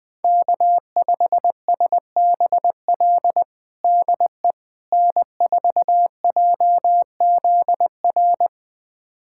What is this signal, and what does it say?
Morse code, heard as a keyed tone, about 20 wpm, 700 Hz. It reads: K5SBL DE N4JZR
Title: USA Callsign DE Callsign - 5 Characters - 20wpm